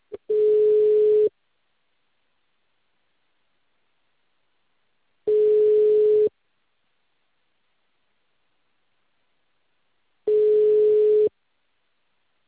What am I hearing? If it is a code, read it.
Ringback tone (Europe, 425 Hz).
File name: out- -2044- - - .28538